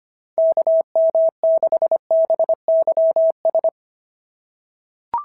Morse code, heard as a keyed tone, about 23 wpm, 650 Hz. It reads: KM6BYS
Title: SOTA QSO Practice 25wpm